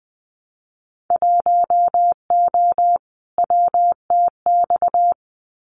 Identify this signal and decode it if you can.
Morse code, heard as a keyed tone, about 20 wpm, 700 Hz. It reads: 1O WTX
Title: Winter Field Day Exchanges Encore 20wpm